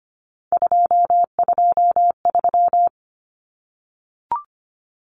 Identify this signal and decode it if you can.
Morse code, heard as a keyed tone, about 25 wpm, 700 Hz. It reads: 223
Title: Common Cut Numbers - 3-Digit 25wpm